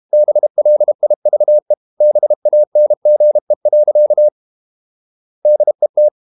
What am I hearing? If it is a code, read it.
Morse code, heard as a keyed tone, about 32 wpm, 600 Hz. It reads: BLIVE BANGE. DET
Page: Ferie 32 wpm